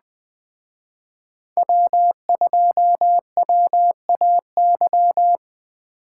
Morse code, heard as a keyed tone, about 19 wpm, 700 Hz. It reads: W2WAY